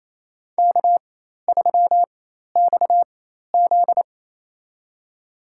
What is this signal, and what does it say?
Morse code, heard as a keyed tone, about 28 wpm, spaced as for 13 wpm, 700 Hz. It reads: K3XZ
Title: USA Call Signs - 4 Characters - 4x Character Spacing - 28wpm